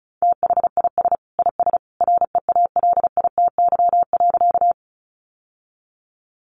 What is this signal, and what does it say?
Morse code, heard as a keyed tone, about 34 wpm, 700 Hz. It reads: THIS IS REALITY.
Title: Sentences from Top 1200 Words 35wpm